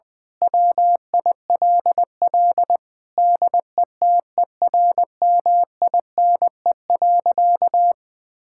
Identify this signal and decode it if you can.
Morse code, heard as a keyed tone, about 20 wpm, 700 Hz. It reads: WILL DETERMINE.
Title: Sentences from Top 1000 Words - Review Part 5 - 20wpm